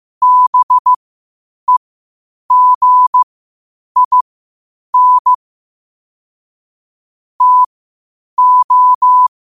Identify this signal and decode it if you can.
Morse code, heard as a keyed tone, about 15 wpm, spaced as for 8 wpm, 1000 Hz. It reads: BEGIN TO
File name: qotdfiles / 8wpm